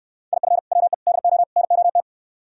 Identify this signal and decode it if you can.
Morse code, heard as a keyed tone, about 69 wpm, 700 Hz. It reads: IF WE DO NOT